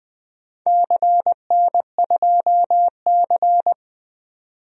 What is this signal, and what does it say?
Morse code, heard as a keyed tone, about 21 wpm, 700 Hz. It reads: CN2C
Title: DXCC-100 Call Signs - 4 Characters - 20wpm